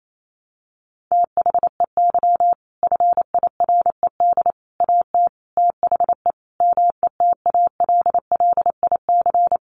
Morse code, heard as a keyed tone, about 28 wpm, 700 Hz. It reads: THEY FIRED AT THE METALLIC
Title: A Sound Of Thunder 28wpm